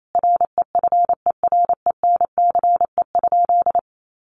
Morse code, heard as a keyed tone, about 28 wpm, 700 Hz. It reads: REFERENCE?
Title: Sentences from Top 1200 Words 28wpm